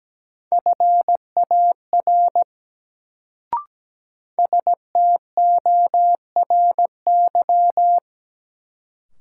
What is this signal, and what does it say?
Morse code, heard as a keyed tone, about 17 wpm, 700 Hz. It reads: FAR STORY
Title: Sets of 3 Words - New Word Drill - 17wpm